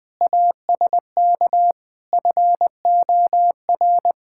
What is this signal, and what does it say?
Morse code, heard as a keyed tone, about 21 wpm, 700 Hz. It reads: ASK FOR
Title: Sentences with 2-Word Phrases - Spoken Beginning - 20wpm